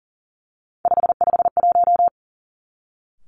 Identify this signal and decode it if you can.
Morse code, heard as a keyed tone, about 37 wpm, 700 Hz. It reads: 551
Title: Three Challenging Letter-Number Combinations 40wpm